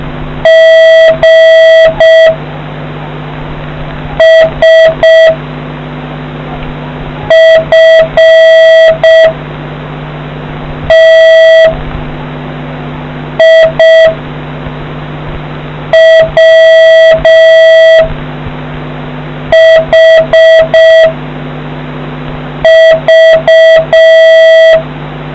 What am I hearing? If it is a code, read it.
Morse code, heard as a keyed tone, about 6 wpm, 650 Hz. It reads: G S F T I W H V